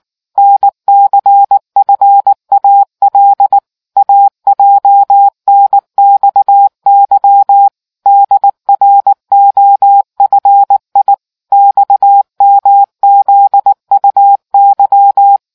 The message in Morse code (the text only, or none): NCFAL AJNXY DROFI XMZUY